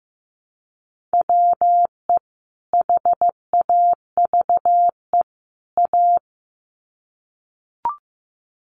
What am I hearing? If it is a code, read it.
Morse code, heard as a keyed tone, about 15 wpm, 700 Hz. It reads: WE HAVE A
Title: Sets of 3 Words 15wpm